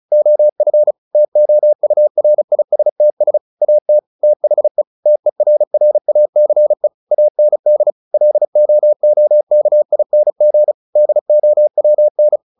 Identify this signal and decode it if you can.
Morse code, heard as a keyed tone, about 35 wpm, 600 Hz. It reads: OF TOURISTS AT THE TERRACE AND LOOKING DOWN
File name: the old man and the sea 2 35wpm